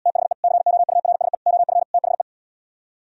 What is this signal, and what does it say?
Morse code, heard as a keyed tone, about 77 wpm, 700 Hz. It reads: THE COURSE OF THE